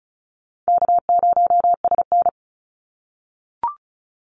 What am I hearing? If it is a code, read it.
Morse code, heard as a keyed tone, about 36 wpm, 700 Hz. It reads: K0SN
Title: Call Signs - 35wpm